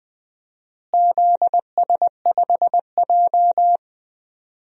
Morse code, heard as a keyed tone, about 20 wpm, 700 Hz. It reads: ZS5J
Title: DXCC-100 Call Signs - 4 Characters - Repeated 3x - 20wpm